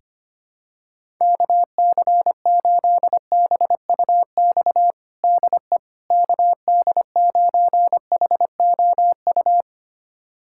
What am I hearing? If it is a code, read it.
Morse code, heard as a keyed tone, about 25 wpm, 700 Hz. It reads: KC8BUX DE KD9HOU
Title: USA Callsign DE Callsign - 6 Characters - 25wpm